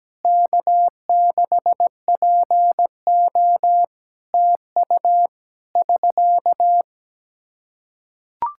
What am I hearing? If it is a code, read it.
Morse code, heard as a keyed tone, about 17 wpm, 700 Hz. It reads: K6PO TU <SK>
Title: Practice QSO - Exchange 4 - Ending - 17wpm